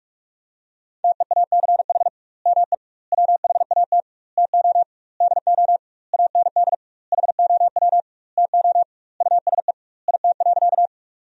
Morse code, heard as a keyed tone, about 45 wpm, 700 Hz. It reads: TEACH ME WHAT TO DO AND HOW TO USE IT.
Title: 2x Word-Spacing - Sentences from Top 500 Words - 45wpm